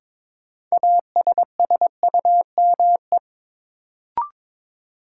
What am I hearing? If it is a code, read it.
Morse code, heard as a keyed tone, about 22 wpm, 700 Hz. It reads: ASSUME
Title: Sentences from Top 1500 Words - New Word Drill Alt - 22wpm